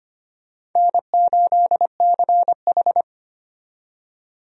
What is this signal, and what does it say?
Morse code, heard as a keyed tone, about 25 wpm, 700 Hz. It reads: N8CH
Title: USA Call Signs - 4 Characters - 25wpm